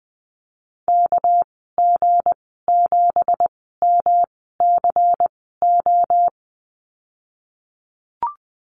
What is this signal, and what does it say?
Morse code, heard as a keyed tone, about 20 wpm, spaced as for 15 wpm, 700 Hz. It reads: KG7MCO
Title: 200 Lesson 32g - Letter k - Call Signs 20wpm